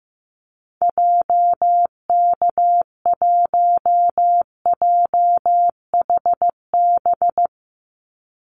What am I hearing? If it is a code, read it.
Morse code, heard as a keyed tone, about 15 wpm, 700 Hz. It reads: JK1JHB
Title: Call Signs Repeated 2x - 15wpm